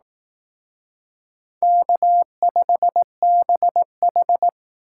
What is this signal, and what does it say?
Morse code, heard as a keyed tone, about 18 wpm, 700 Hz. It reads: K5BH
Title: Speed Racing - USA Call Signs 15wpm